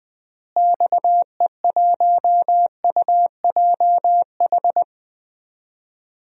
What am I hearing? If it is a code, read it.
Morse code, heard as a keyed tone, about 20 wpm, 700 Hz. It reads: XE1UJH
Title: DXCC-100 Call Signs - 6 Characters - Repeated 3x - 20wpm